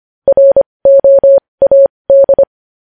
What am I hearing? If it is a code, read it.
Morse code, heard as a keyed tone, about 25 wpm, spaced as for 20 wpm, 550 Hz. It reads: ROAD
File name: podcast / News headlines-1x-25x20